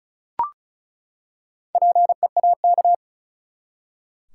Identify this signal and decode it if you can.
Morse code, heard as a keyed tone, about 34 wpm, 700 Hz. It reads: PEAK
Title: Four Letter Words - 35wpm